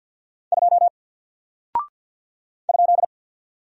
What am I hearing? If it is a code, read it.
Morse code, heard as a keyed tone, about 50 wpm, 700 Hz. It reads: 2 ?
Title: Single Character (Letter:Number:Punctuation:Prosign) Alt 50wpm